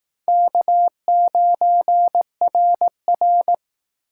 Morse code, heard as a keyed tone, about 18 wpm, 700 Hz. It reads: K9RR
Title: Speed Racing - USA Call Signs 15wpm